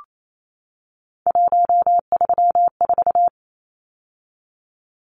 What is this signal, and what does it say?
Morse code, heard as a keyed tone, about 28 wpm, 700 Hz. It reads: 134
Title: Numbers - 3-Digit 28wpm